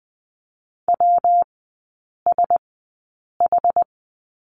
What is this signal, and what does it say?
Morse code, heard as a keyed tone, about 20 wpm, spaced as for 8 wpm, 700 Hz. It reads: WSH
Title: 114 Lesson 19e - Letter w - Three Characters 20wpm